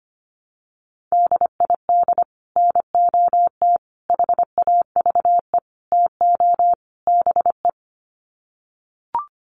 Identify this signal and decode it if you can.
Morse code, heard as a keyed tone, about 25 wpm, 700 Hz. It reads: DID NOT HAVE TO BE
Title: Sets of 5 Words Encore2 25wpm